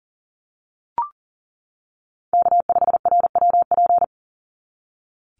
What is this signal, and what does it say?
Morse code, heard as a keyed tone, about 40 wpm, 700 Hz. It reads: K5RWP
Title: NA SOTA Chaser Callsigns 40wpm